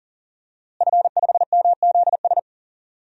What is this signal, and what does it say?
Morse code, heard as a keyed tone, about 40 wpm, 700 Hz. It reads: F5MZS
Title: DXCC-100 Call Signs - 5 Characters - Repeated 3x - 40wpm